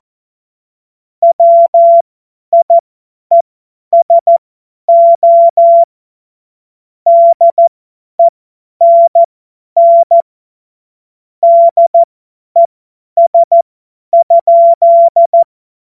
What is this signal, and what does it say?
Morse code, heard as a keyed tone, about 14 wpm, spaced as for 10 wpm, 680 Hz. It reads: WIESO DENN DES?
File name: Morsekurs / Witze / Witz 21. März 2023 50 F70